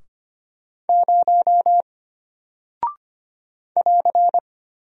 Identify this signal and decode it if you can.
Morse code, heard as a keyed tone, about 25 wpm, 700 Hz. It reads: 0 <AR>
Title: Single Character (Letter:Number:Punctuation:Prosign) Alt 25wpm